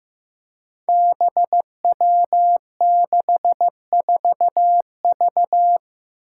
Morse code, heard as a keyed tone, about 15 wpm, 700 Hz. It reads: BW64V
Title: Five Challenging Letter-Number Combinations 15wpm